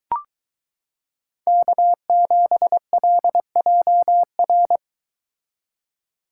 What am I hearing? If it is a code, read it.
Morse code, heard as a keyed tone, about 23 wpm, 700 Hz. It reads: K7LJR